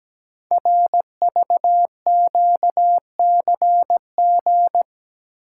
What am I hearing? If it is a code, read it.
Morse code, heard as a keyed tone, about 17 wpm, 700 Hz. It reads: RVQCG
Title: Code Groups - 5 Letters - 17wpm